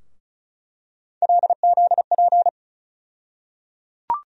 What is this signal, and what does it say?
Morse code, heard as a keyed tone, about 36 wpm, 700 Hz. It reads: LZP
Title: USA Call Sign Postfixes - 3 Letters 35wpm